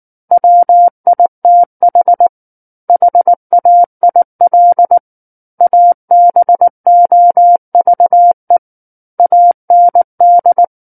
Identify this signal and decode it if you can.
Morse code, heard as a keyed tone, about 19 wpm, 700 Hz. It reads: WITH HAIL ABOVE AND
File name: Book One / 10-Chapter 10 - 19 WPM